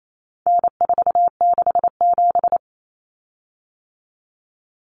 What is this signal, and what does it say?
Morse code, heard as a keyed tone, about 28 wpm, 700 Hz. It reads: N467
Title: Common Cut Numbers - 4-Digit 28wpm